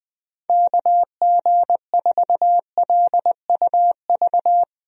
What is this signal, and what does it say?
Morse code, heard as a keyed tone, about 21 wpm, 700 Hz. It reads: KG4LUV
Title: USA Call Signs - 6 Characters - Repeated 3x - 20wpm